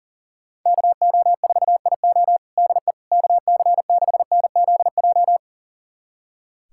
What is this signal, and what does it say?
Morse code, heard as a keyed tone, about 40 wpm, 700 Hz. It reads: KO4IO DE KC6NZJ
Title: USA Callsign DE Callsign 40wpm